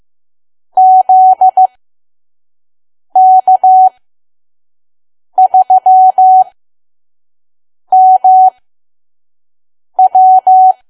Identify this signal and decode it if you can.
Morse code, heard as a keyed tone, about 15 wpm, spaced as for 5 wpm, 750 Hz. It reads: ZK3MW
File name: Archive / 5 WPM files / PM